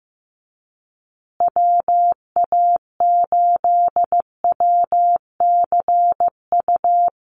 Morse code, heard as a keyed tone, about 15 wpm, 700 Hz. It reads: WA8WCU
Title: USA Call Signs Repeated 3x 15wpm